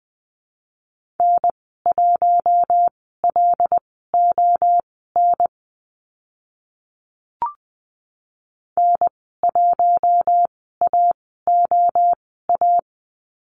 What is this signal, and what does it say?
Morse code, heard as a keyed tone, about 20 wpm, spaced as for 14 wpm, 700 Hz. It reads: N1LON N1AOA
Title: 076 Lesson 13g - Letter L - Call Signs 20wpm